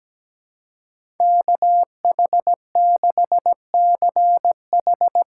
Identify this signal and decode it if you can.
Morse code, heard as a keyed tone, about 17 wpm, 700 Hz. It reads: KH6CH
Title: USA Call Signs - 6x Character Spacing - 17wpm